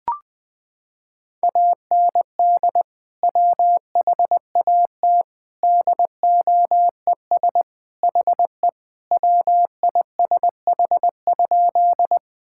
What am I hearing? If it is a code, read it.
Morse code, heard as a keyed tone, about 21 wpm, 700 Hz. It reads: AND WHAT DOES HE WISH?
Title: Sentences from Top 1600 Words 20wpm